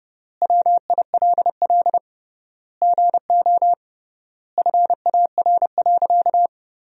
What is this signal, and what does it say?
Morse code, heard as a keyed tone, about 30 wpm, 700 Hz. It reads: WILL GO FAR.
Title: 3x Word-Spacing - Sentences from Top 400 Words - 30wpm